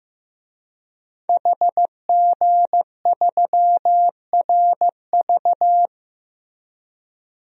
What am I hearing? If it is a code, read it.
Morse code, heard as a keyed tone, about 15 wpm, 700 Hz. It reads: HG3RV
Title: DXCC-100 Call Signs - 5 Characters - Repeated 2x - 15wpm